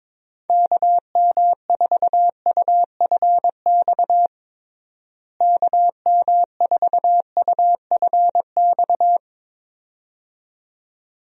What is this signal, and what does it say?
Morse code, heard as a keyed tone, about 21 wpm, 700 Hz. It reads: KM4UFX KM4UFX
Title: USA Call Signs - 6 Characters - Repeated 2x - 22wpm